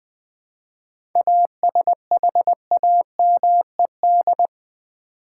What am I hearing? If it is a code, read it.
Morse code, heard as a keyed tone, about 20 wpm, 700 Hz. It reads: ASHAMED
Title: 7 Letter Words - First 3 Letters Given - 20wpm